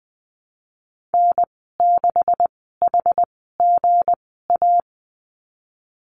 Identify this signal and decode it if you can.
Morse code, heard as a keyed tone, about 20 wpm, spaced as for 15 wpm, 700 Hz. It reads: N6HGA
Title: 173 Lesson 28g - Number 9 - Call Signs 20wpm